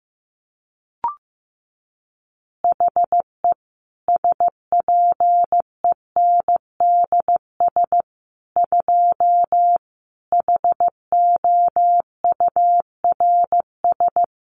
Morse code, heard as a keyed tone, about 15 wpm, 700 Hz. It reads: HE SPENDS 2 HOURS
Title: Additional Easy Phrases 15wpm